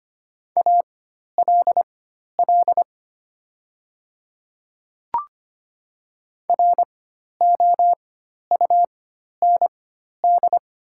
Morse code, heard as a keyed tone, about 25 wpm, spaced as for 12 wpm, 700 Hz. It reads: ALL ROUND
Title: 4x Character Spacing - Top 200 Words - 25wpm